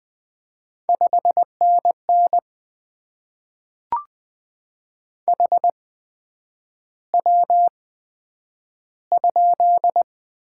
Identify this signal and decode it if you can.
Morse code, heard as a keyed tone, about 20 wpm, 700 Hz. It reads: 5 N N H W ?
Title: Top 20 QSO Elements - 8x Character Spacing - 20wpm